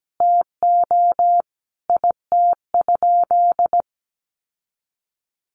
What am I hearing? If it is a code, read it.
Morse code, heard as a keyed tone, about 17 wpm, 700 Hz. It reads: TO IT?
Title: Sentences from Top 1300 Words 17wpm